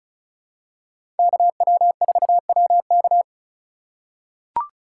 Morse code, heard as a keyed tone, about 36 wpm, 700 Hz. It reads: KW4WK
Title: USA Call Signs - 5 Characters - 3x Character Spacing - 35wpm